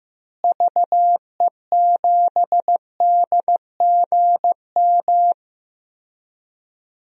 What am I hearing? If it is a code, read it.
Morse code, heard as a keyed tone, about 15 wpm, 700 Hz. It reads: VE7DGM
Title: DXCC-100 Call Signs - 6 Characters - Repeated 2x - 15wpm